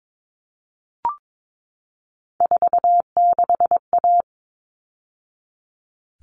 Morse code, heard as a keyed tone, about 22 wpm, 700 Hz. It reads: 46A
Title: All Cut Numbers - 3-Digit 22wpm